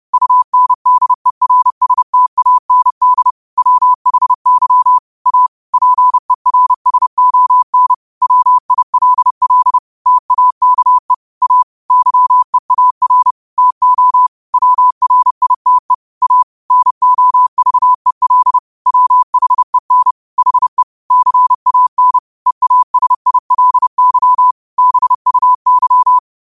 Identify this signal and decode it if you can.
Morse code, heard as a keyed tone, about 30 wpm, 1000 Hz. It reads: UNDERSTAND WHY A PERSON WILL TAKE A YEAR TO WRITE A NOVEL WHEN HE CAN EASILY BUY
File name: qotdfiles / 30wpm